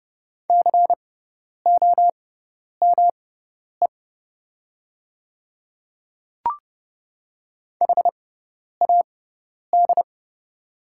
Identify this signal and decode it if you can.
Morse code, heard as a keyed tone, about 31 wpm, spaced as for 10 wpm, 700 Hz. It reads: COME HAD
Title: 6x Character Spacing - Top 500 Words - 30wpm